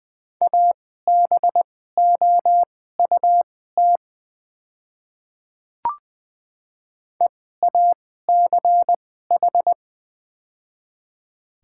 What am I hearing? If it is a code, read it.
Morse code, heard as a keyed tone, about 20 wpm, spaced as for 15 wpm, 690 Hz. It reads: ABOUT EACH
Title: 2x Character Spacing - Top 100 Words - 20wpm